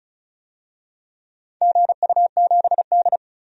Morse code, heard as a keyed tone, about 34 wpm, 700 Hz. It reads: GU7D
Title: DXCC-100 Call Signs - 4 Characters - 35wpm